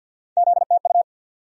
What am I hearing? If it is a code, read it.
Morse code, heard as a keyed tone, about 49 wpm, 700 Hz. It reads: ZTU
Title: USA Call Sign Postfixes - 3 Letters 50wpm